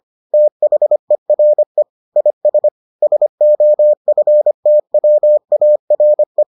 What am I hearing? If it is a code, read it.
Morse code, heard as a keyed tone, about 25 wpm, 600 Hz. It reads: THERE IS SOFTWARE